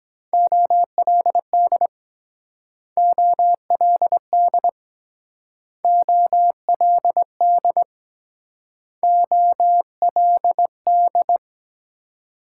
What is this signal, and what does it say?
Morse code, heard as a keyed tone, about 22 wpm, 700 Hz. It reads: OLD OLD OLD OLD